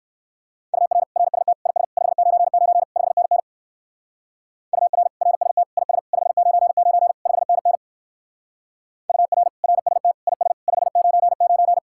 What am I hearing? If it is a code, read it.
Morse code, heard as a keyed tone, about 55 wpm, 700 Hz. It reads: UR RST IS 599 5NN UR RST IS 599 5NN UR RST IS 599
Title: Speed-Racing Common QSO Phrases Encore - 45wpm